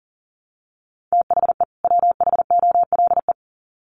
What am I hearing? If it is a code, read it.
Morse code, heard as a keyed tone, about 40 wpm, 700 Hz. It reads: THE WHOLE
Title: Sets of 2 Words 40wpm